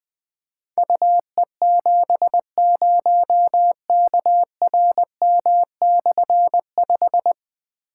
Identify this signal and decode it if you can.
Morse code, heard as a keyed tone, about 20 wpm, 700 Hz. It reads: UE70KRM/5
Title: Difficult Call Signs 20wpm